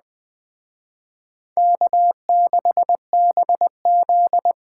Morse code, heard as a keyed tone, about 20 wpm, 700 Hz. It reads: K6BZ
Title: Speed Racing - USA 1x2 Call Signs 17wpm